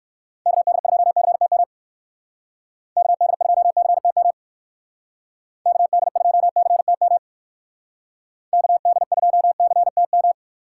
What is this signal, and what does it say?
Morse code, heard as a keyed tone, about 64 wpm, 700 Hz. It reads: KD2CTM KD2CTM KD2CTM KD2CTM